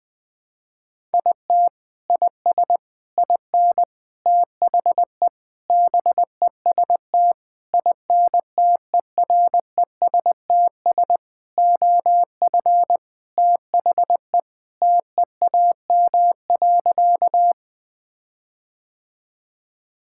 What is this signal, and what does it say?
Morse code, heard as a keyed tone, about 21 wpm, 700 Hz. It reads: IT IS IN THE BEST INTERESTS OF THE TEAM.
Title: Sentences with 6-Word Phrases - Spoken Beginning - 20wpm